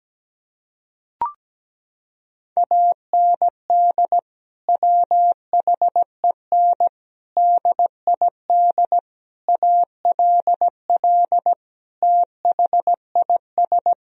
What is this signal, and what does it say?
Morse code, heard as a keyed tone, about 17 wpm, 700 Hz. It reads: AND WHEN DID ALL THIS
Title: Sentences from Top 1000 Words - Review Part 4 - 17wpm